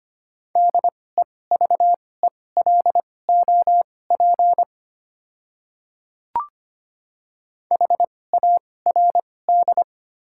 Morse code, heard as a keyed tone, about 25 wpm, spaced as for 18 wpm, 700 Hz. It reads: DEVELOP HARD
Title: 2x Character Spacing - Top 500 Words - 25wpm